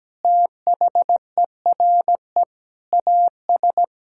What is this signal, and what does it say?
Morse code, heard as a keyed tone, about 17 wpm, 700 Hz. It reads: THERE AS
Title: Sentences from Top 1600 Words 17wpm